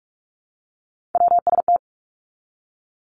Morse code, heard as a keyed tone, about 46 wpm, 700 Hz. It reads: WST
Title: Three Challenging Letter-Number Combinations 45wpm